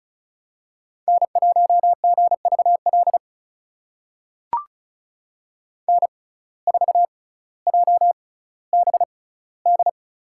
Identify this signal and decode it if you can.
Morse code, heard as a keyed tone, about 34 wpm, spaced as for 18 wpm, 700 Hz. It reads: N1GVL N4JBD
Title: USA Call Signs - 5 Characters - 6x Character Spacing - 35wpm